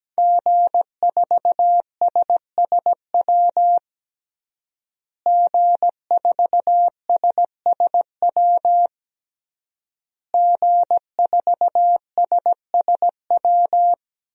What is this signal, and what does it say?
Morse code, heard as a keyed tone, about 17 wpm, 700 Hz. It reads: G4SSW G4SSW G4SSW
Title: DXCC-100 Call Signs - 5 Characters - Repeated 3x - 17wpm